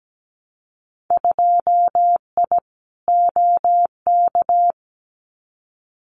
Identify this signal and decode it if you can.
Morse code, heard as a keyed tone, about 17 wpm, 700 Hz. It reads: 2I OK
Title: Winter Field Day Exchanges Encore 17wpm